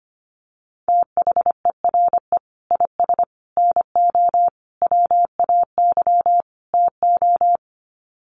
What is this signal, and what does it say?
Morse code, heard as a keyed tone, about 25 wpm, 700 Hz. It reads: THERE IS NO WAY TO
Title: Sets of 5 Words Encore 25wpm